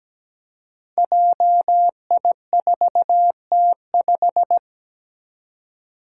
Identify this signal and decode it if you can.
Morse code, heard as a keyed tone, about 17 wpm, 700 Hz. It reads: JI4T5
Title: Five Challenging Letter-Number Combinations 17wpm